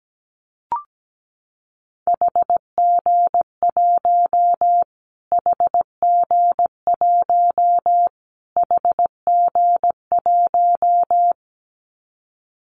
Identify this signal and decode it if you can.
Morse code, heard as a keyed tone, about 17 wpm, 700 Hz. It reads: HG1 HG1 HG1
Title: Callsign Prefixes for Top 25 DXCC - Repeated 3x - 17wpm